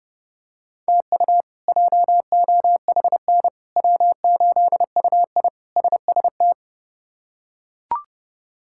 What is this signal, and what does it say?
Morse code, heard as a keyed tone, about 31 wpm, 700 Hz. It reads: TU JOHN W8UI SST
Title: K1USN SST QSO Practice - 30wpm